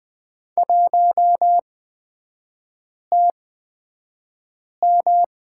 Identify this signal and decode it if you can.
Morse code, heard as a keyed tone, about 20 wpm, spaced as for 5 wpm, 700 Hz. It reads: 1TM
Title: 255 Lesson 41e - Period - Three Characters 20wpm